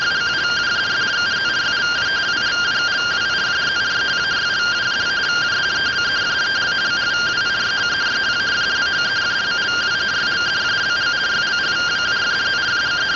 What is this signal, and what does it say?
Two-tone teleprinter signal, about 170 Hz shift.
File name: findings / CT 1 Scout 0001.gen-11 rtty rx